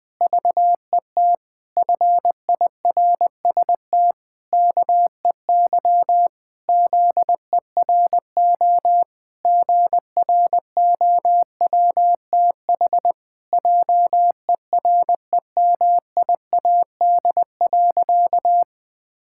Morse code, heard as a keyed tone, about 21 wpm, 700 Hz. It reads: VET FIRST KEY ZERO GROWTH JEREMIAD.
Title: Pangram Challenge v2 - 20wpm